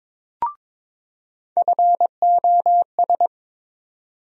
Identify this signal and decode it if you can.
Morse code, heard as a keyed tone, about 22 wpm, 700 Hz. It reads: FOS
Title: USA Call Sign Postfixes - 3 Letters 22wpm